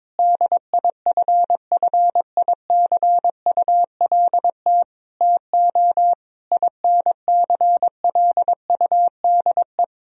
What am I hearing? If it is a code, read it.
Morse code, heard as a keyed tone, about 22 wpm, 700 Hz. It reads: DIFFICULT TO INCLUDE
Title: Sentences from Top 1000 Words - Review Part 3 - 22wpm